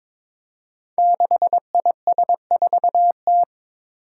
Morse code, heard as a keyed tone, about 22 wpm, 700 Hz. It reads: 6IS4T
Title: Five Challenging Letter-Number Combinations 22wpm